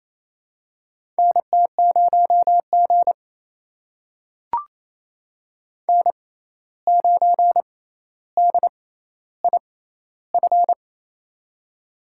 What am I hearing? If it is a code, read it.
Morse code, heard as a keyed tone, about 28 wpm, spaced as for 13 wpm, 700 Hz. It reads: NT0G N9DIF
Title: USA Call Signs - 6x Character Spacing - 28wpm